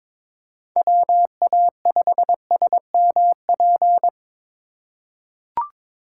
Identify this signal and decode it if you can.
Morse code, heard as a keyed tone, about 22 wpm, 700 Hz. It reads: WA5SMP
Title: USA Call Signs - 6 Characters - 2x Character Spacing - 22wpm